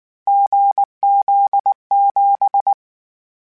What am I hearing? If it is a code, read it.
Morse code, heard as a keyed tone, about 19 wpm, 800 Hz. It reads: GZ7
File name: Difficult Combos / Difficult Combos-mp3s / GZ GZ7 @18